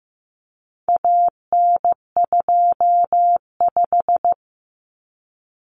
Morse code, heard as a keyed tone, about 15 wpm, 700 Hz. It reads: AN25
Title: Common Cut Numbers - 4-Digit 15wpm